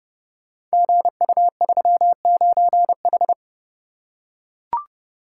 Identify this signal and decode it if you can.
Morse code, heard as a keyed tone, about 30 wpm, 700 Hz. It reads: GU39H
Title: Code Groups - 5 Letter:Numbers - 30wpm